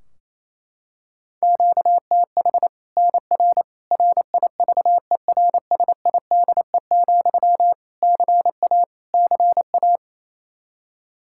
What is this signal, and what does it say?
Morse code, heard as a keyed tone, about 28 wpm, 700 Hz. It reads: QTH NR RIVERSIDE, CA CA
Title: Practice QSO - Exchange 1 - Beginning - 28wpm